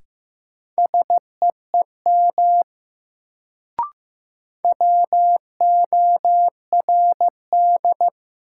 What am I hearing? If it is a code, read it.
Morse code, heard as a keyed tone, about 15 wpm, 700 Hz. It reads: SEEM WORD
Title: Sentences from Top 300 Words - New Word Drill - 15wpm